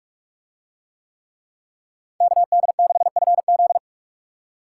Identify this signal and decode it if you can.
Morse code, heard as a keyed tone, about 44 wpm, 700 Hz. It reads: KD6FZ